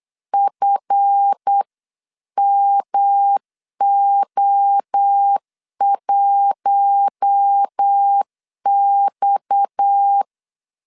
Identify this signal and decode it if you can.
Morse code, heard as a keyed tone, about 8 wpm, 800 Hz. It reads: FMO1X
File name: LEZIONI / 10 SIGNS / 10B INT 8wpm 3534